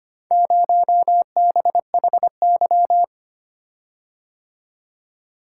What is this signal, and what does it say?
Morse code, heard as a keyed tone, about 25 wpm, 700 Hz. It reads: 0BHY
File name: practice / 25wpm / Four Letter-Number Combinations - 25wpm